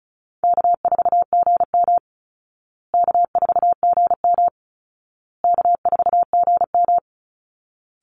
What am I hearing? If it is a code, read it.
Morse code, heard as a keyed tone, about 34 wpm, 700 Hz. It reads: K4GM K4GM K4GM
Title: Easy Call Signs Repeated 3x - 35wpm